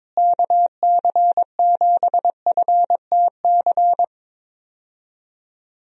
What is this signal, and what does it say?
Morse code, heard as a keyed tone, about 21 wpm, 700 Hz. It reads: KC7FTC
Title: USA Call Signs - 6 Characters - Repeated 3x - 22wpm